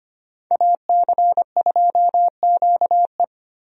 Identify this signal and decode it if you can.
Morse code, heard as a keyed tone, about 25 wpm, 700 Hz. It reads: AC2QE
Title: USA Call Signs - 5 Characters - Repeated 2x - 25wpm